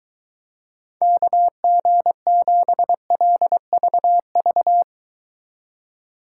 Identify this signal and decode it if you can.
Morse code, heard as a keyed tone, about 23 wpm, 700 Hz. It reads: KG7LVV